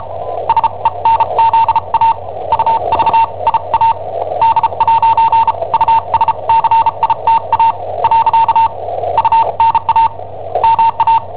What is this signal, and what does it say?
Morse code, heard as a keyed tone, about 34 wpm, spaced as for 29 wpm, 910 Hz, with noise. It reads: SENZA UVIA D'USCITA . UNA MA
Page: Cassetta esercizi preparata da I7OHP per IW5ELP